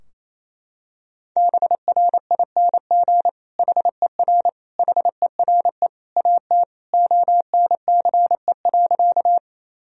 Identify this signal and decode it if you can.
Morse code, heard as a keyed tone, about 27 wpm, 700 Hz. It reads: BRING HER HERE AT ONCE.
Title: Sentences from Top 1000 Words - Review Part 3 - 28wpm